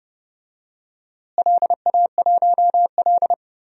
Morse code, heard as a keyed tone, about 30 wpm, 700 Hz. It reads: LA1L
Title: DXCC-100 Call Signs - 4 Characters - Repeated 3x - 30wpm